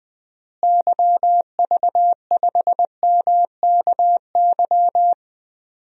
Morse code, heard as a keyed tone, about 20 wpm, 700 Hz. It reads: YV5MKY
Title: DXCC-100 Call Signs - 6 Characters - Repeated 3x - 20wpm